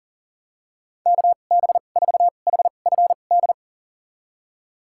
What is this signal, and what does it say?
Morse code, heard as a keyed tone, about 37 wpm, spaced as for 28 wpm, 700 Hz. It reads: KB4HFD
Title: USA Call Signs - 2x Character Spacing - 40wpm